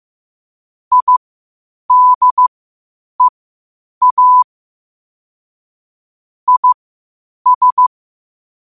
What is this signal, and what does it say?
Morse code, heard as a keyed tone, about 15 wpm, spaced as for 8 wpm, 1000 Hz. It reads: IDEA IS
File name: qotdfiles / 8wpm